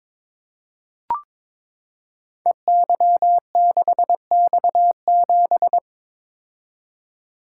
Morse code, heard as a keyed tone, about 21 wpm, 700 Hz. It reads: EY6X7
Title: Code Groups - 5 Letter:Numbers - 22wpm